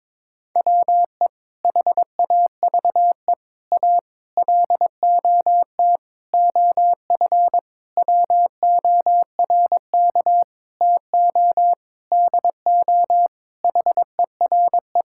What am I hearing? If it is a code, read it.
Morse code, heard as a keyed tone, about 22 wpm, 700 Hz. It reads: WE HAVE A LOT OF WORK TO DO HERE
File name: Sentences with 7-Word Phrases - 22wpm